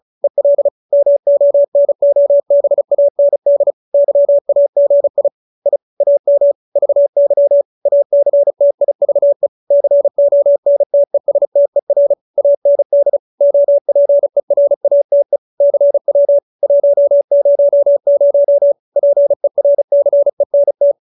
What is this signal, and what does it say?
Morse code, heard as a keyed tone, about 34 wpm, 570 Hz. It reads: EL MONOBAND YAGI I AM VY ACTIVE CONTESTER AND OPERATE CW 100 PERCENT